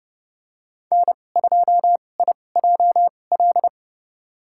Morse code, heard as a keyed tone, about 32 wpm, spaced as for 22 wpm, 700 Hz. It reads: N2IJL
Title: USA Call Signs - 5 Characters - 2x Character Spacing - 30wpm